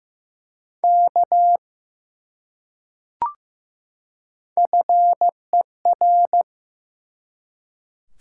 Morse code, heard as a keyed tone, about 15 wpm, 700 Hz. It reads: K FER
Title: Top 180 QSO Elements 15wpm